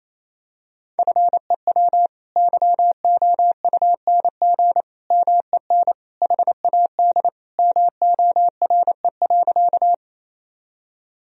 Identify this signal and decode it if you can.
Morse code, heard as a keyed tone, about 28 wpm, 700 Hz. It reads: FEW YOUNG MEN HAD MORE.
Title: Sentences from Top 1000 Words - Review Part 1 - 28wpm